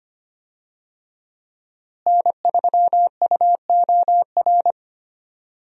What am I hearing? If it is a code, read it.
Morse code, heard as a keyed tone, about 25 wpm, 700 Hz. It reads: N3UOR
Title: USA Call Signs - 5 Characters - Repeated 2x - 25wpm